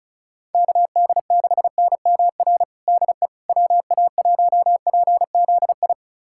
Morse code, heard as a keyed tone, about 36 wpm, 700 Hz. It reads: KD6NMR DE WA1PZI
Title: USA Callsign DE Callsign 35wpm